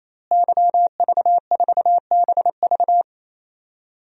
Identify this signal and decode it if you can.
Morse code, heard as a keyed tone, about 27 wpm, 700 Hz. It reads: YV4BV
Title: DXCC-100 Call Signs - 5 Characters - Repeated 3x - 28wpm